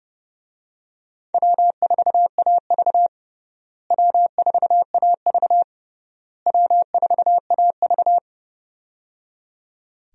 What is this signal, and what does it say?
Morse code, heard as a keyed tone, about 29 wpm, 700 Hz. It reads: W4AV W4AV W4AV
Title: USA Call Signs - 4 Characters - Repeated 3x - 30wpm